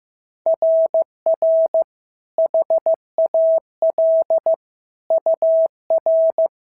Morse code, heard as a keyed tone, about 14 wpm, 650 Hz. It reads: RR HAL UR
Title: SOTA QSO Practice 15wpm